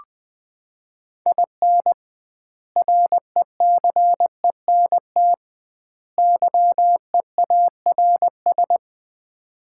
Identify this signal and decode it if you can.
Morse code, heard as a keyed tone, about 20 wpm, 700 Hz. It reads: IN RECENT YEARS
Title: 2x Word-Spacing - Sets of 3 Words - 20wpm